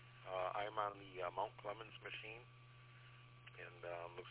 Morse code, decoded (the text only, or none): none